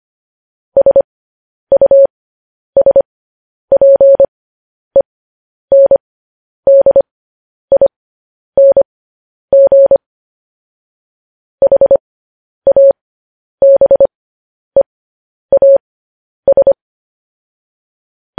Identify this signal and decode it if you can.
Morse code, heard as a keyed tone, about 25 wpm, spaced as for 10 wpm, 550 Hz. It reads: SUSPENDING HABEAS
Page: News Headlines in Morse Code at 10 WPM